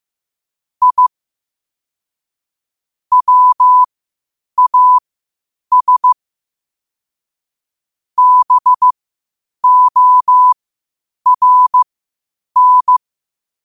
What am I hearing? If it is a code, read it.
Morse code, heard as a keyed tone, about 15 wpm, spaced as for 8 wpm, 1000 Hz. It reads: I WAS BORN